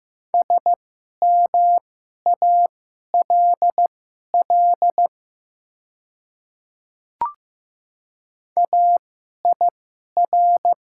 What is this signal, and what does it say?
Morse code, heard as a keyed tone, about 15 wpm, spaced as for 11 wpm, 700 Hz. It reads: SMALL AIR
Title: 2x Character Spacing - Top 200 Words - 15wpm